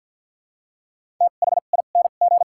Morse code, heard as a keyed tone, about 53 wpm, spaced as for 37 wpm, 700 Hz. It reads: THING
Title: 2x Character Spacing - Top 200 Words - 50wpm